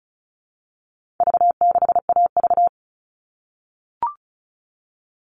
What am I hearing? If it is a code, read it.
Morse code, heard as a keyed tone, about 34 wpm, 700 Hz. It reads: V6AV
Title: All Cut Numbers - 4-Digit 35wpm